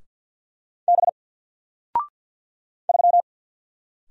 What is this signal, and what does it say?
Morse code, heard as a keyed tone, about 48 wpm, 700 Hz. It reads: B 3
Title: Single Character (Letter:Number:Punctuation:Prosign) Alt 50wpm